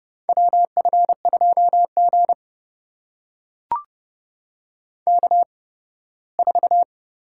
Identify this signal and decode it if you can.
Morse code, heard as a keyed tone, about 30 wpm, spaced as for 18 wpm, 700 Hz. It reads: WF2G K4
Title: USA Call Signs - 8x Character Spacing - 30wpm